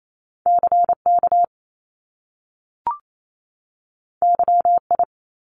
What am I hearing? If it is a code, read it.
Morse code, heard as a keyed tone, about 28 wpm, 700 Hz. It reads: C K Y I
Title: Top Two Letter Combinations in Words 28wpm